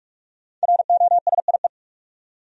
Morse code, heard as a keyed tone, about 46 wpm, 700 Hz. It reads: ROSIE
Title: USA Common Names - Max 5 Letters - 45wpm